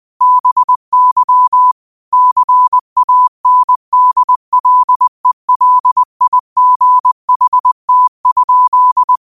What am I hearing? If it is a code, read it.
Morse code, heard as a keyed tone, about 20 wpm, 1000 Hz. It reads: BY CANDLELIGHT?